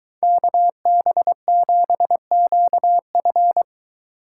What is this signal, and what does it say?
Morse code, heard as a keyed tone, about 21 wpm, 700 Hz. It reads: KB7QF